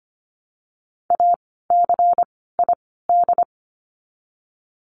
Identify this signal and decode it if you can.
Morse code, heard as a keyed tone, about 25 wpm, 700 Hz. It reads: A C I D
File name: course / 25:10wpm / 094 Lesson 16f - Letter c - Words 25wpm